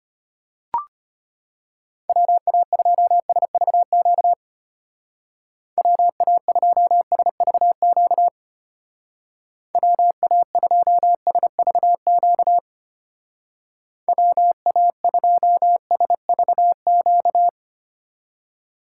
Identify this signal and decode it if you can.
Morse code, heard as a keyed tone, about 32 wpm, 700 Hz. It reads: WA2SVQ WA2SVQ WA2SVQ WA2SVQ